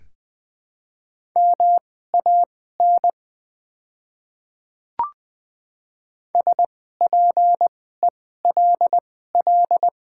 Morse code, heard as a keyed tone, about 20 wpm, spaced as for 14 wpm, 700 Hz. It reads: MAN SPELL
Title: 2x Character Spacing - Top 300 Words - 20wpm